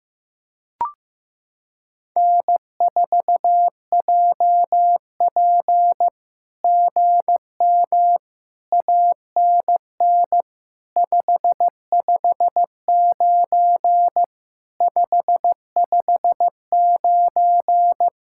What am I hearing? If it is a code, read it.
Morse code, heard as a keyed tone, about 15 wpm, 700 Hz. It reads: N4JP GM ANN 559 559
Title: SOTA QSO Practice 15wpm